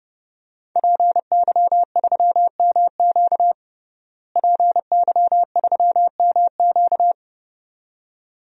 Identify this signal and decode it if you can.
Morse code, heard as a keyed tone, about 31 wpm, 700 Hz. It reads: PY3MQ PY3MQ
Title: DXCC-100 Call Signs - 5 Characters - Repeated 2x - 30wpm